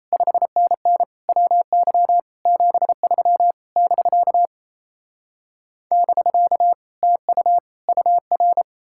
Morse code, heard as a keyed tone, about 37 wpm, 700 Hz. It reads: 5NN WY 73 <BK> <BK> TU UR